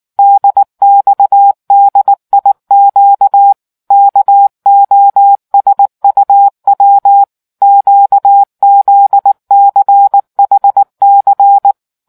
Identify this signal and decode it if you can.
Morse code, heard as a keyed tone, about 19 wpm, 800 Hz. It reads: DXDIQ KOSUW QZCHC